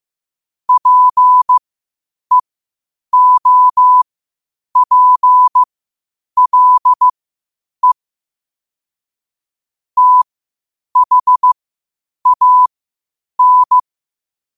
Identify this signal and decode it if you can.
Morse code, heard as a keyed tone, about 15 wpm, spaced as for 8 wpm, 1000 Hz. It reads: PEOPLE THAN
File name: qotdfiles / 8wpm